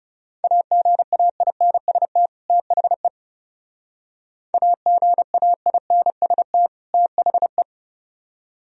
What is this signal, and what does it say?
Morse code, heard as a keyed tone, about 37 wpm, 700 Hz. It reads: AGAINST THE AGAINST THE
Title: Speed-Racing - Sets of 2 Words 30wpm